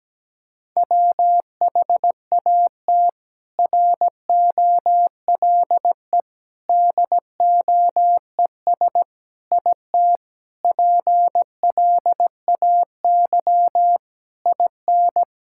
Morse code, heard as a keyed tone, about 17 wpm, 700 Hz. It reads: WHAT ROLE DOES IT PLAY IN
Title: Sentences from Top 1000 Words - Review Part 6 - 17wpm